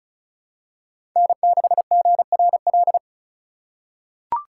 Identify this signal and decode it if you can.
Morse code, heard as a keyed tone, about 36 wpm, 690 Hz. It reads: N6GRL
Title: USA Call Signs - 5 Characters - 3x Character Spacing - 35wpm